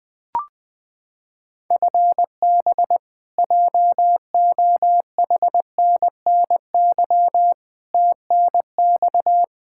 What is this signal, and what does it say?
Morse code, heard as a keyed tone, about 20 wpm, 700 Hz. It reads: FB JOHNNY TNX
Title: Practice QSO - Exchange 3 - More Info - 20wpm